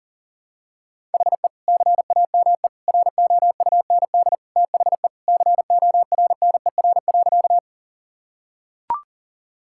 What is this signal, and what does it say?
Morse code, heard as a keyed tone, about 40 wpm, 700 Hz. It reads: HE CAME ROUND THE CORNER.
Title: Sentences from Top 1600 Words 40wpm